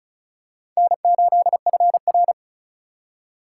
Morse code, heard as a keyed tone, about 34 wpm, 700 Hz. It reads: N8FR
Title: USA Call Signs - 4 Characters - 6x Character Spacing - 35wpm